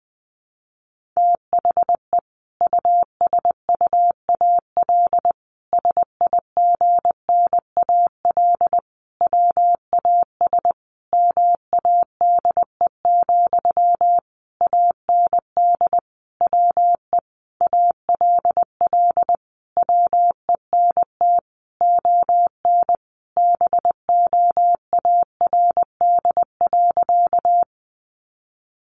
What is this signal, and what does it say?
Morse code, heard as a keyed tone, about 20 wpm, 700 Hz. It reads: THE USUAL SIGNAL WAS MADE, AND WE ALL WENT ON BOARD.
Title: Sentences from Top 1400 Words 20wpm